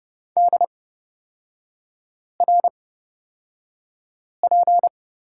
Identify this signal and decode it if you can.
Morse code, heard as a keyed tone, about 30 wpm, spaced as for 5 wpm, 700 Hz. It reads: DRP
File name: Lesson 36e - BT - Three Characters 30wpm